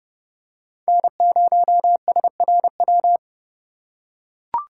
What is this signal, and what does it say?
Morse code, heard as a keyed tone, about 30 wpm, 700 Hz. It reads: N0SRW
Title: USA Call Signs - 5 Characters - 2x Character Spacing - 30wpm